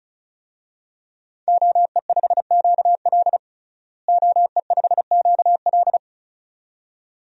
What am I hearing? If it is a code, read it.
Morse code, heard as a keyed tone, about 36 wpm, 700 Hz. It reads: OE5QL OE5QL
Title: DXCC-100 Call Signs - 5 Characters - Repeated 2x - 35wpm